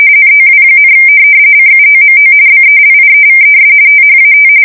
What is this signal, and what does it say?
RTTY, about 45.45 baud. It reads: NOW, CUOMO BELIEVES THAT
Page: 170 Shift 2125/2295